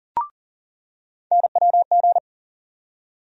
Morse code, heard as a keyed tone, about 40 wpm, 700 Hz. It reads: NWG
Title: USA Call Sign Postfixes - 3 Letters 40wpm